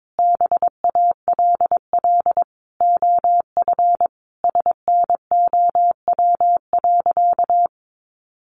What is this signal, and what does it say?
Morse code, heard as a keyed tone, about 22 wpm, 700 Hz. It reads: BALL OF SNOW.
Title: Easy Phrases 22wpm